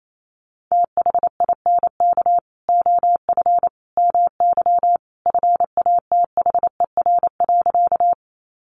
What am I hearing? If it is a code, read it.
Morse code, heard as a keyed tone, about 28 wpm, 700 Hz. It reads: THINK OF MY FATHER.
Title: Sentences from Top 300 Words 28wpm